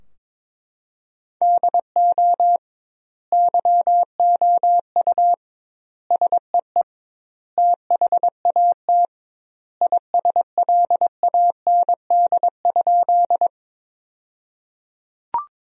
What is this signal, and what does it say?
Morse code, heard as a keyed tone, about 21 wpm, 700 Hz. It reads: DO YOU SEE THAT ISLAND?
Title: 2x Word-Spacing - Sentences from Top 500 Words - 22wpm